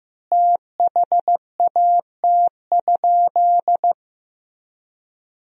Morse code, heard as a keyed tone, about 15 wpm, 700 Hz. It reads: THAT?
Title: 3x Word-Spacing - Sentences from Top 300 Words - 15wpm